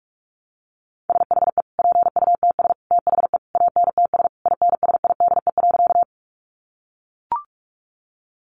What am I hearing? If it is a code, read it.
Morse code, heard as a keyed tone, about 44 wpm, 700 Hz. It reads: SHE PUTS THE ANTS INSIDE.
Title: Easy Phrases 45wpm